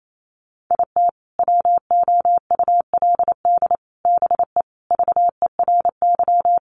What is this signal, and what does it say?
Morse code, heard as a keyed tone, about 28 wpm, 700 Hz. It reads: IT WOULD BE VERY
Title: Sentences from Top 200 Words 28wpm